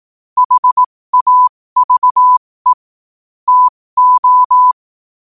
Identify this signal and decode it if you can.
Morse code, heard as a keyed tone, about 18 wpm, spaced as for 15 wpm, 1000 Hz. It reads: HAVE TO
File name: 15wpm